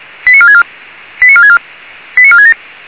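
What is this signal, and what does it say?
Five-tone selective call (ZVEI), heard as quick runs of five tones, tones of about 70 ms. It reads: 89464 89464 89467
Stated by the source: ZVEI five-tone call received on an SDR, channel 464.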